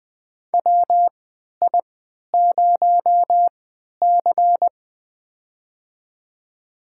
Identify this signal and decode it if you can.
Morse code, heard as a keyed tone, about 21 wpm, spaced as for 11 wpm, 700 Hz. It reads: WI0C
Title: USA Call Signs - 4 Characters - 3x Character Spacing - 20wpm